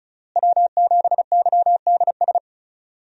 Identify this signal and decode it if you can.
Morse code, heard as a keyed tone, about 36 wpm, 700 Hz. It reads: W7YDS
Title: USA Call Signs - 5 Characters - Repeated 3x - 35wpm